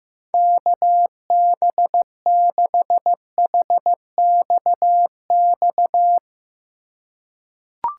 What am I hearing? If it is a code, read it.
Morse code, heard as a keyed tone, about 15 wpm, 700 Hz. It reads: KB6HXX
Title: USA Call Signs - 6 Characters - 4x Character Spacing - 15wpm